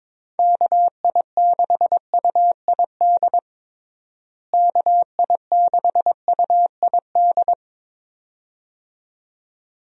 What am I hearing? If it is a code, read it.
Morse code, heard as a keyed tone, about 21 wpm, 700 Hz. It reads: KI6UID KI6UID
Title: USA Call Signs - 6 Characters - Repeated 2x - 22wpm